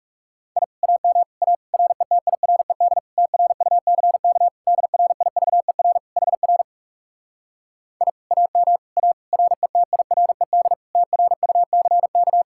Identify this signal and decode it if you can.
Morse code, heard as a keyed tone, about 44 wpm, 700 Hz. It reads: I AM A RETIRED TRUCK DRIVER HR I AM A RETIRED TRUCK